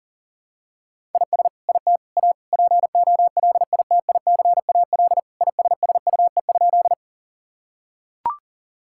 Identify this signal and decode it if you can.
Morse code, heard as a keyed tone, about 40 wpm, 700 Hz. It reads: IS IT A POLITICAL ISSUE?
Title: Sentences from Top 1000 Words - Review Part 6 - 40wpm